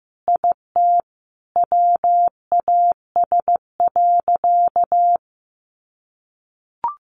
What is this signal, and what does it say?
Morse code, heard as a keyed tone, about 15 wpm, 700 Hz. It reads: IT WAS.
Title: Sentences from Top 100 Words 15wpm